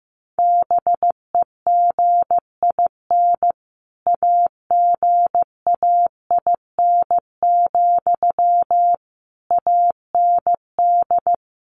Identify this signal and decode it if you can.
Morse code, heard as a keyed tone, about 15 wpm, 700 Hz. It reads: BEGIN AGAIN, AND
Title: Sentences from Top 800 Words 15wpm